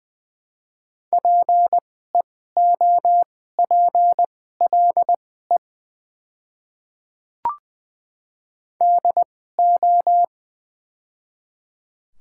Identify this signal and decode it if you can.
Morse code, heard as a keyed tone, about 20 wpm, spaced as for 14 wpm, 700 Hz. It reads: PEOPLE DO
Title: 2x Character Spacing - Top 200 Words - 20wpm